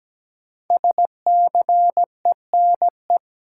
Morse code, heard as a keyed tone, about 17 wpm, 700 Hz. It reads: SCENE
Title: Sentences from Top 1100 Words - New Word Drill Alt - 17wpm